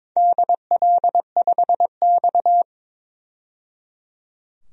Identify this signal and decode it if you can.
Morse code, heard as a keyed tone, about 22 wpm, 700 Hz. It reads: DL5X
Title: DXCC-100 Call Signs - 4 Characters - 22wpm